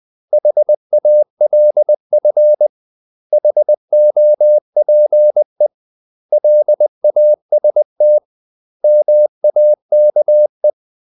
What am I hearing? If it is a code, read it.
Morse code, heard as a keyed tone, about 20 wpm, 600 Hz. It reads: HALF HOPE LAST MAKE
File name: CW / Practice / 500MostCommonEnglishWords20